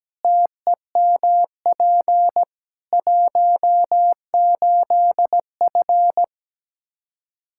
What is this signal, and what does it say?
Morse code, heard as a keyed tone, about 17 wpm, 700 Hz. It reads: TEMP 18F
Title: Practice QSO - Exchange 2 - Info - 17wpm